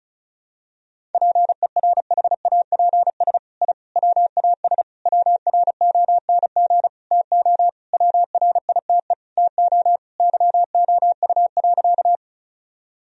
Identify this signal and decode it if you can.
Morse code, heard as a keyed tone, about 34 wpm, 700 Hz. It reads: PERHAPS I WAS WRONG TO WRITE TO YOU.
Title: Sentences from Top 1000 Words - Review Part 1 - 35wpm